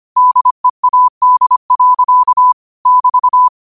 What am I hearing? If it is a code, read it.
Morse code, heard as a keyed tone, about 25 wpm, 1000 Hz. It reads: DEAD. <BT>
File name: qotdfiles / 25wpm